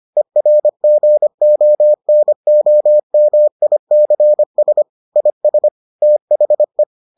Morse code, heard as a keyed tone, about 25 wpm, 600 Hz. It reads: ERGONOMICS IS THE